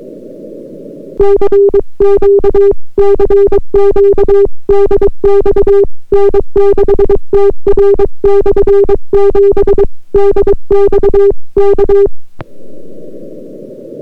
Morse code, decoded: CQ CQ DX N6TR/7 DX K